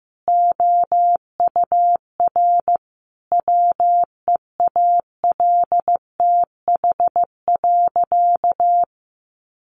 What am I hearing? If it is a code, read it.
Morse code, heard as a keyed tone, about 15 wpm, 700 Hz. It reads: OUR WEALTH.
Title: Sentences from Top 1200 Words 15wpm